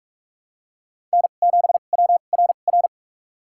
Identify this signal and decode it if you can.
Morse code, heard as a keyed tone, about 46 wpm, spaced as for 32 wpm, 700 Hz. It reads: N7WRR
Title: USA Call Signs - 5 Characters - 2x Character Spacing - 45wpm